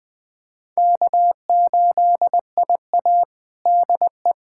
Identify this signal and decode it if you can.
Morse code, heard as a keyed tone, about 21 wpm, 700 Hz. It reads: K8IA DE
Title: Practice QSO - Exchange 4 - Ending - 20wpm